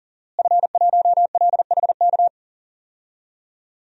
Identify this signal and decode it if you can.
Morse code, heard as a keyed tone, about 40 wpm, 700 Hz. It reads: F1LHK
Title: DXCC-100 Call Signs - 5 Characters - Repeated 2x - 40wpm